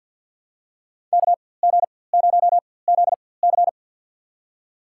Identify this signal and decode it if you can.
Morse code, heard as a keyed tone, about 53 wpm, spaced as for 24 wpm, 700 Hz. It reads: KG0ZC
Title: USA Call Signs - 4x Character Spacing - 50wpm